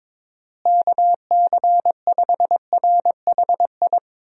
Morse code, heard as a keyed tone, about 22 wpm, 700 Hz. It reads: KC5RHI
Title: USA Call Signs - 6 Characters - Repeated 3x - 22wpm